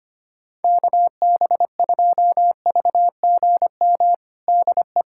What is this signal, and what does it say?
Morse code, heard as a keyed tone, about 25 wpm, 700 Hz. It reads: KB2VGM DE
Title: USA Callsign DE Callsign - 6 Characters - 25wpm